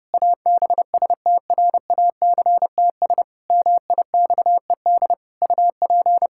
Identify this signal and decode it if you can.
Morse code, heard as a keyed tone, about 30 wpm, 700 Hz. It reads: ABSTRACTS MIXED UP
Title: Pangram Challenge v2 - 30wpm